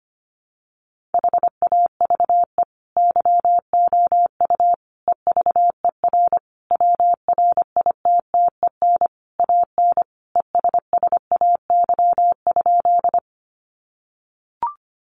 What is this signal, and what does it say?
Morse code, heard as a keyed tone, about 25 wpm, 700 Hz. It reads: HAVE YOU EVER WRITTEN AN ESSAY?
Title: Sentences from Top 1500 Words 25wpm